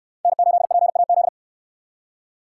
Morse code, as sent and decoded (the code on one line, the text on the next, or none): -. ---.. .--. .- --..
N8PAZ